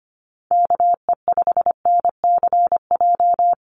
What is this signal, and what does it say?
Morse code, heard as a keyed tone, about 25 wpm, 700 Hz. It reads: KE5NCJ
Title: USA Call Signs Repeated 3x 25wpm